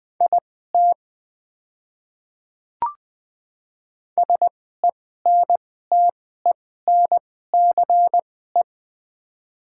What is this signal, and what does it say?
Morse code, heard as a keyed tone, about 20 wpm, spaced as for 14 wpm, 700 Hz. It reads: IT SENTENCE
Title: 2x Character Spacing - Top 300 Words - 20wpm